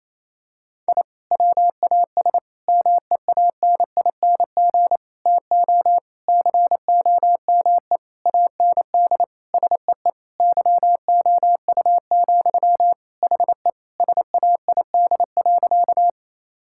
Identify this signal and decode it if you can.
Morse code, heard as a keyed tone, about 28 wpm, 700 Hz. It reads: I WAS MEANING TO COME AND SEE YOU, HE SAID.
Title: Sentences from Top 1000 Words - Review Part 6 - 28wpm